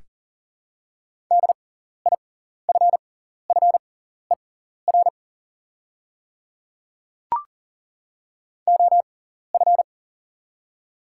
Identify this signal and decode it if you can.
Morse code, heard as a keyed tone, about 40 wpm, spaced as for 14 wpm, 700 Hz. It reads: DIFFER OF
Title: 6x Character Spacing - Top 200 Words - 40wpm